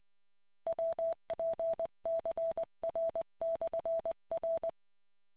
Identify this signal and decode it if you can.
Morse code, heard as a keyed tone, about 20 wpm, 660 Hz. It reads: WPCR/R